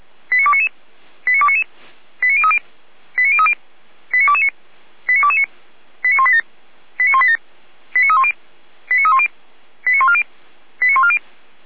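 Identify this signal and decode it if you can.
Five-tone selective call (ZVEI), heard as quick runs of five tones, tones of about 70 ms. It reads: 89299 89299 89030 89030 89229 89229 89187 89187 89310 89310 89140 89140
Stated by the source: Signal received via SDR, ZVEI five-tone.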